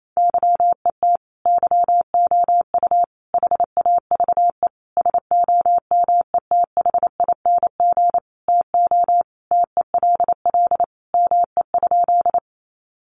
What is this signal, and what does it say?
Morse code, heard as a keyed tone, about 28 wpm, 700 Hz. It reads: YET YOU HAVE SOMETHING TO TELL ME?
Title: Sentences from Top 600 Words 28wpm